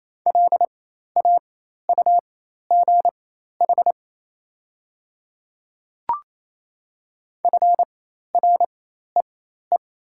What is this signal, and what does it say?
Morse code, heard as a keyed tone, about 28 wpm, spaced as for 13 wpm, 700 Hz. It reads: LAUGH FREE
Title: 4x Character Spacing - Top 500 Words - 28wpm